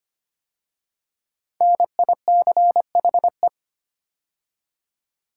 Morse code, heard as a keyed tone, about 25 wpm, 700 Hz. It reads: NICHE
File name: 5 Letter Words - First 3 Letters Given - 25wpm